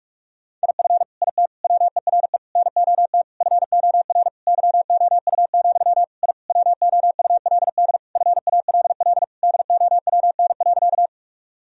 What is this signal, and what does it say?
Morse code, heard as a keyed tone, about 46 wpm, 700 Hz. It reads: IF IT WERE NOT FOR YOU, I WOULD FALL DOWN.
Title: Sentences from Top 1000 Words - Review Part 2 - 45wpm